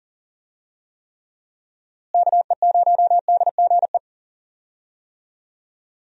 Morse code, heard as a keyed tone, about 40 wpm, 700 Hz. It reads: KE0DGE
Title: USA Call Signs - 6 Characters - Repeated 2x - 40wpm